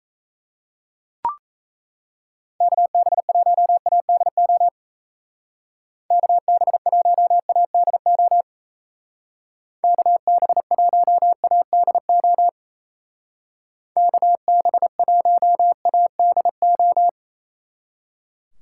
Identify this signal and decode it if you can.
Morse code, heard as a keyed tone, about 34 wpm, 700 Hz. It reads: KB1ADO KB1ADO KB1ADO KB1ADO
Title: Speed Racing - USA 2x3 Call Signs 28wpm